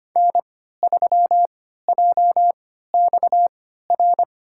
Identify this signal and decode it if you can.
Morse code, heard as a keyed tone, about 25 wpm, spaced as for 14 wpm, 700 Hz. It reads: N3JXR
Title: USA Call Signs - 5 Characters - 3x Character Spacing - 25wpm